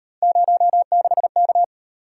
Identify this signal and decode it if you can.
Morse code, heard as a keyed tone, about 35 wpm, 700 Hz. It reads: 06K